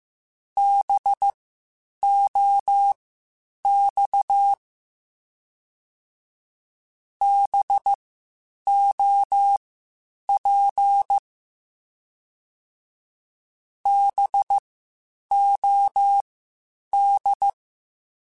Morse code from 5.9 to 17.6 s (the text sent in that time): BOP BOD